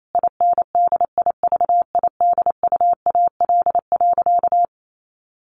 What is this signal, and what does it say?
Morse code, heard as a keyed tone, about 28 wpm, 700 Hz. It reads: INDIVIDUAL.
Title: Sentences from Top 1500 Words 28wpm